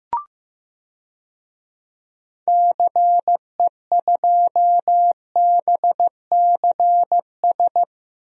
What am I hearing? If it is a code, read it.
Morse code, heard as a keyed tone, about 15 wpm, 700 Hz. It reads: CE2BCS
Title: DXCC-100 Call Signs - 6 Characters - Repeated 2x - 15wpm